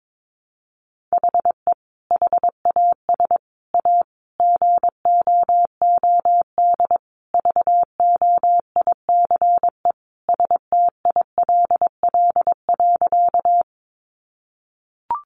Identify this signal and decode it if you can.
Morse code, heard as a keyed tone, about 22 wpm, 700 Hz. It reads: HE HAS A GOOD VOICE STILL.
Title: Sentences from Top 500 Words 22wpm